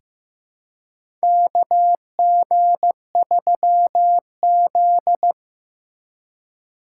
Morse code, heard as a keyed tone, about 15 wpm, 700 Hz. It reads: KG3Z
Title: USA Call Signs - 4 Characters - 2x Character Spacing - 15wpm